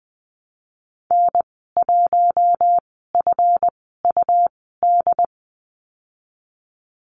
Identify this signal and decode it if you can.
Morse code, heard as a keyed tone, about 20 wpm, spaced as for 15 wpm, 700 Hz. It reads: N1FUD
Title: 154 Lesson 25g - Letter p - Call Signs 20wpm